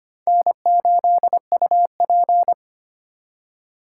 Morse code, heard as a keyed tone, about 31 wpm, 700 Hz. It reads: N8UP
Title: Speed Racing - USA Call Signs 25wpm